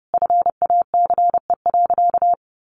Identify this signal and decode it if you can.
Morse code, heard as a keyed tone, about 28 wpm, 700 Hz. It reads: FACE.
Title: A Sound Of Thunder 30wpm